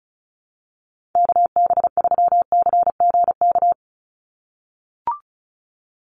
Morse code, heard as a keyed tone, about 34 wpm, 700 Hz. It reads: KB3CGK
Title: USA Call Signs 35wpm